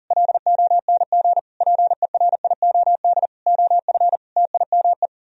Morse code, heard as a keyed tone, about 50 wpm, 700 Hz. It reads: LONG PERIOD OF TIME